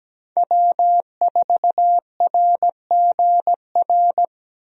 Morse code, heard as a keyed tone, about 17 wpm, 700 Hz. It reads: W4RGR
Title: USA Call Signs - 5 Characters - Repeated 3x - 17wpm